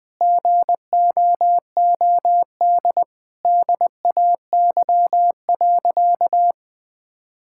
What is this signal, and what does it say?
Morse code, heard as a keyed tone, about 20 wpm, 700 Hz. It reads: GOOD DAY.
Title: Sentences with 4-Word Phrases - 20wpm